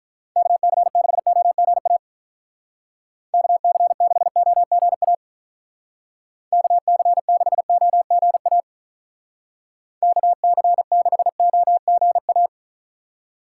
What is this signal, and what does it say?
Morse code, heard as a keyed tone, about 43 wpm, 700 Hz. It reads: KC6OGA KC6OGA KC6OGA KC6OGA
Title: Speed Racing - USA 2x3 Call Signs 35wpm